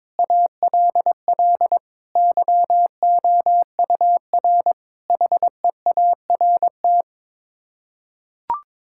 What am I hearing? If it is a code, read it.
Morse code, heard as a keyed tone, about 22 wpm, 700 Hz. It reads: ALL YOUR HEART
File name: Sets of 9 Words - 22wpm